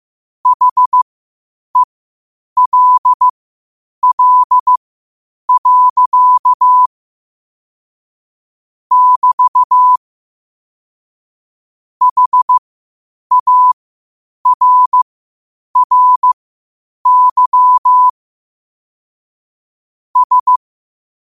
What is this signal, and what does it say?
Morse code, heard as a keyed tone, about 15 wpm, spaced as for 8 wpm, 1000 Hz. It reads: HELL. <BT> HARRY S
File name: qotdfiles / 8wpm